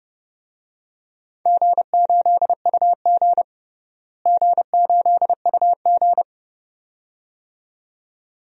Morse code, heard as a keyed tone, about 30 wpm, 700 Hz. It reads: G8UG G8UG
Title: DXCC-100 Call Signs - 4 Characters - Repeated 2x - 30wpm